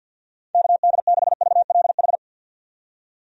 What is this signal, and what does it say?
Morse code, heard as a keyed tone, about 53 wpm, 700 Hz. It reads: KD6VLH
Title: USA Call Signs - 6 Characters - 50wpm